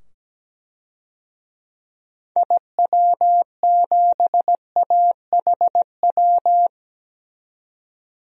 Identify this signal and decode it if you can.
Morse code, heard as a keyed tone, about 17 wpm, 700 Hz. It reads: IW7AHW
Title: DXCC-100 Call Signs - 6 Characters - 17wpm